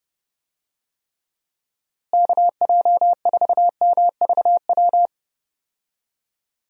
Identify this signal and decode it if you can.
Morse code, heard as a keyed tone, about 30 wpm, 700 Hz. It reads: KJ4MVW
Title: USA Call Signs - 6 Characters - Repeated 3x - 30wpm